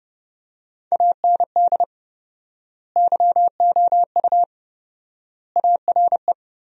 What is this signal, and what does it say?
Morse code, heard as a keyed tone, about 30 wpm, 700 Hz. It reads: AND YOU ARE
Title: 4x Word-Spacing - Sentences from Top 400 Words - 30wpm